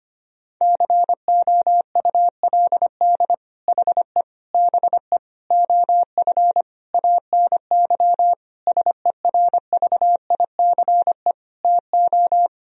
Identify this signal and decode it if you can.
Morse code, heard as a keyed tone, about 25 wpm, 700 Hz. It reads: COULD HE BE OF ANY SERVICE TO
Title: Sentences from Top 1000 Words - Review Part 4 - 25wpm